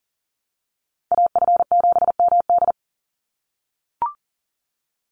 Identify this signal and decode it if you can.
Morse code, heard as a keyed tone, about 40 wpm, 700 Hz. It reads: AF7MD
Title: NA SOTA Chaser Callsigns 40wpm